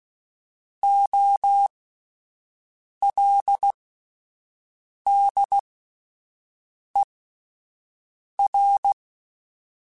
Morse code, heard as a keyed tone, about 16 wpm, spaced as for 5 wpm, 800 Hz. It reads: OLDER
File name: Alphabetical / Alphabetical mp3s / Os @5